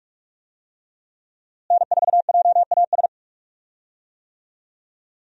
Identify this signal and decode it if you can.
Morse code, heard as a keyed tone, about 45 wpm, 700 Hz. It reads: N4JAS
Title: USA Call Signs - 5 Characters - Repeated 3x - 45wpm